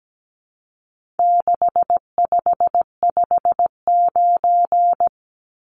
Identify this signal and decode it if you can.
Morse code, heard as a keyed tone, about 17 wpm, 700 Hz. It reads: 6559
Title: Numbers - 4-Digit 17wpm